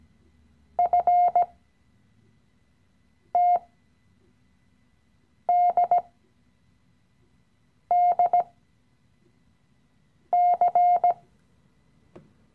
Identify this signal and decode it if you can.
Morse code, heard as a keyed tone, about 18 wpm, spaced as for 4 wpm, 710 Hz. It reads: FTDDC
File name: CW-Online-Kurs / Anfänger / Lektion 016